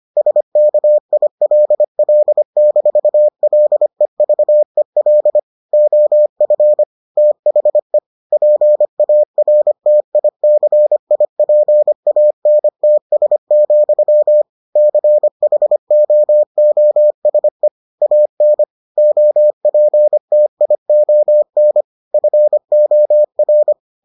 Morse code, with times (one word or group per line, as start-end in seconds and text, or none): 0.2-5.4 SKILL-LEVEL
5.7-6.8 OF
7.2-8.0 THE
8.3-14.4 PARTICIPANTS,
14.8-17.7 CHOOSE
18.0-18.6 AN
19.0-21.8 OPTION
22.1-23.7 FOR